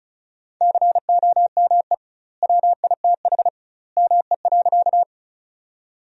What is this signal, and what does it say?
Morse code, heard as a keyed tone, about 34 wpm, 700 Hz. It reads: COME WITH ME.
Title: 2x Word-Spacing - Sentences from Top 200 Words - 35wpm